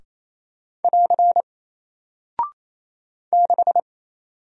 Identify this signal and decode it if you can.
Morse code, heard as a keyed tone, about 28 wpm, 700 Hz. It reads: <AR> 6
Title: Single Character (Letter:Number:Punctuation:Prosign) Alt 28wpm